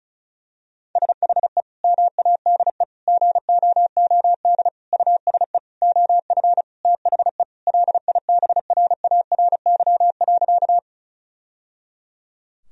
Morse code, heard as a keyed tone, about 36 wpm, 700 Hz. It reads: SHE MADE GOOD USE OF THE LIBRARY.
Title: Sentences from Top 1000 Words - Review Part 4 - 35wpm